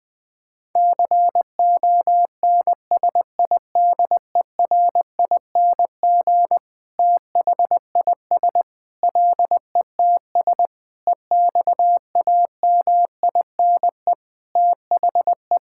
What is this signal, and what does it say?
Morse code, heard as a keyed tone, about 21 wpm, 700 Hz. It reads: CONSIDERING THIS LETS EXAMINE THE
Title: Sentences with 9-Word Phrases - Spoken Beginning - 20wpm